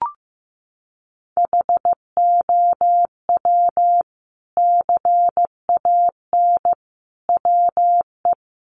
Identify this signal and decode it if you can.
Morse code, heard as a keyed tone, about 15 wpm, 700 Hz. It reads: HOW CAN WE
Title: Sentences from Top 600 Words 15wpm